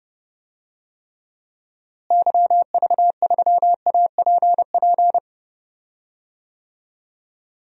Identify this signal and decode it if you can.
Morse code, heard as a keyed tone, about 30 wpm, 700 Hz. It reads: YV3APP
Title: DXCC-100 Call Signs - 6 Characters - Repeated 2x - 30wpm